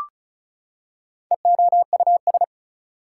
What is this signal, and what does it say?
Morse code, heard as a keyed tone, about 36 wpm, 700 Hz. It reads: EOUS
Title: Prefixes and Suffixes 35wpm